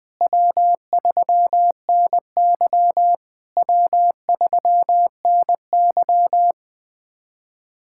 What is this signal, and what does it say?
Morse code, heard as a keyed tone, about 20 wpm, 700 Hz. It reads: W3NY W3NY
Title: POTA QSO Practice - 20wpm